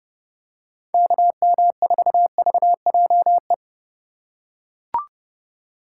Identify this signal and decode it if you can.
Morse code, heard as a keyed tone, about 30 wpm, 700 Hz. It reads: KM4VJE
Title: USA Call Signs - 6 Characters - 6x Character Spacing - 30wpm